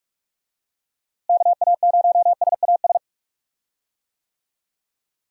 Morse code, heard as a keyed tone, about 45 wpm, 700 Hz. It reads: KA0SAS
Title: USA Call Signs - 6 Characters - Repeated 3x - 45wpm